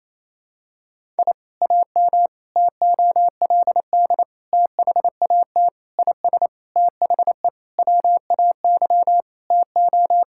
Morse code, heard as a keyed tone, about 28 wpm, 700 Hz. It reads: I AM TOLD THAT IS THE WAY TO
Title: Sentences from Top 1000 Words - Review Part 5 - 28wpm